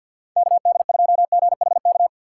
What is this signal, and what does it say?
Morse code, heard as a keyed tone, about 50 wpm, 700 Hz. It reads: KD2GHK
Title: USA Call Signs - 3x Character Spacing - 50wpm